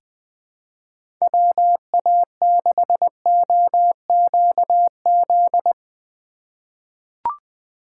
Morse code, heard as a keyed tone, about 20 wpm, 700 Hz. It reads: WA6OQZ
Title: USA Call Signs - 6x Character Spacing - 20wpm